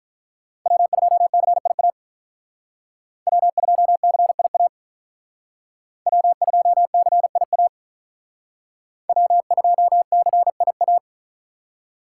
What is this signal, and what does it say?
Morse code, heard as a keyed tone, about 45 wpm, 700 Hz. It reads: W2CIA W2CIA W2CIA W2CIA